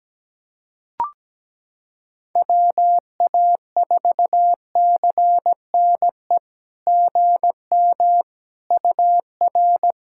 Morse code, heard as a keyed tone, about 17 wpm, 700 Hz. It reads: WA4CNE GM UR
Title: SOTA QSO Practice 17wpm